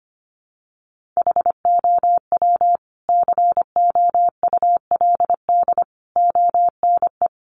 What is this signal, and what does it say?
Morse code, heard as a keyed tone, about 25 wpm, 700 Hz. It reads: HOW COULD ONE
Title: Sentences from Top 1100 Words 25wpm